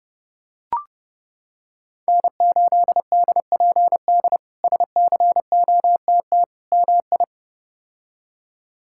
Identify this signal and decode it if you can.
Morse code, heard as a keyed tone, about 30 wpm, 700 Hz. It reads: N8DPD SCOTT MI
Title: K1USN SST QSO Practice - 30wpm